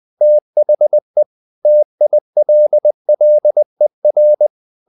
Morse code, heard as a keyed tone, about 20 wpm, 600 Hz. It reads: THE TILLER
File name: the old man and the sea part 2 20wpm